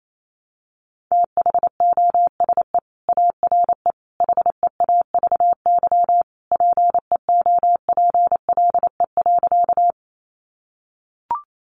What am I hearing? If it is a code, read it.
Morse code, heard as a keyed tone, about 28 wpm, 700 Hz. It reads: THOSE ARE HEAVY PEOPLE.
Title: Sentences from Top 1100 Words 28wpm